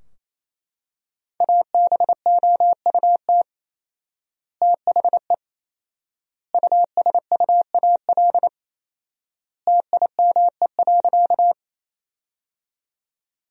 Morse code, heard as a keyed tone, about 28 wpm, 700 Hz. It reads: ABOUT THE USUAL TIME.
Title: 4x Word-Spacing - Sentences from Top 400 Words - 28wpm